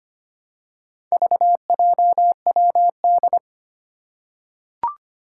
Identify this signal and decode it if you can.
Morse code, heard as a keyed tone, about 25 wpm, 700 Hz. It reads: VJWD
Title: Four Challenging Letter-Number Combinations 25wpm